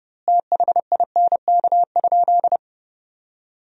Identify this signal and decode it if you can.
Morse code, heard as a keyed tone, about 30 wpm, 700 Hz. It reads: THINK?
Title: 2x Word-Spacing - Sentences from Top 200 Words - 30wpm